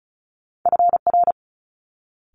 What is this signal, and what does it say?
Morse code, heard as a keyed tone, about 36 wpm, 700 Hz. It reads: FR
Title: Multi-letter Phonemes Quick Fire 35wpm